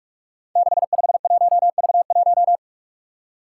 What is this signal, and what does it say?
Morse code, heard as a keyed tone, about 41 wpm, 700 Hz. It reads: 651V1
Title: Five Challenging Letter-Number Combinations 45wpm